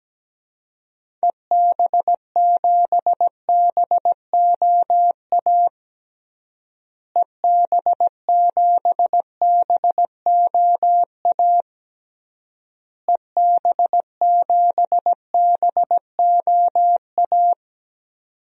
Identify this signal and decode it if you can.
Morse code, heard as a keyed tone, about 17 wpm, 700 Hz. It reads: EB7BOA EB7BOA EB7BOA
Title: DXCC-100 Call Signs - 6 Characters - Repeated 3x - 17wpm